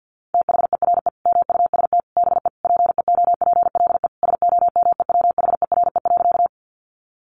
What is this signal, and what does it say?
Morse code, heard as a keyed tone, about 48 wpm, 700 Hz. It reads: THERE MUST BE PEOPLE SOMEWHERE.
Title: Sentences from Top 1400 Words 50wpm